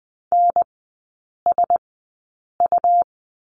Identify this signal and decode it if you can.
Morse code, heard as a keyed tone, about 20 wpm, spaced as for 8 wpm, 700 Hz. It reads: NSU
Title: 107 Lesson 18e - Letter m - Three Characters 20wpm